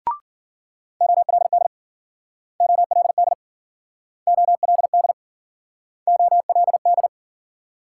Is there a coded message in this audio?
Yes: Morse code from 1.0 to 7.1 s, heard as a keyed tone, about 53 wpm, 700 Hz. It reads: OLD OLD OLD OLD